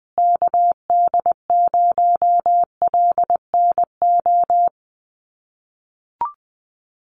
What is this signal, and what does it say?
Morse code, heard as a keyed tone, about 20 wpm, 700 Hz. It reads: KD0LNO
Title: USA Call Signs Repeated 3x 20wpm